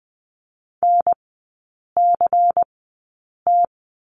Morse code, heard as a keyed tone, about 20 wpm, spaced as for 8 wpm, 690 Hz. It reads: NCT
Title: 093 Lesson 16e - Letter c - Three Characters 20wpm